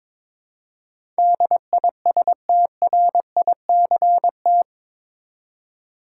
Morse code, heard as a keyed tone, about 22 wpm, 700 Hz. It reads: DISTRICT
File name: practice / 22wpm / Eight Letter Words - 22wpm